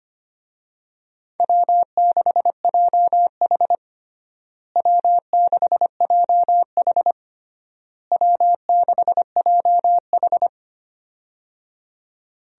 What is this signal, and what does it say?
Morse code, heard as a keyed tone, about 25 wpm, 700 Hz. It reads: W6JH W6JH W6JH
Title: USA Call Signs - 4 Characters - Repeated 3x - 25wpm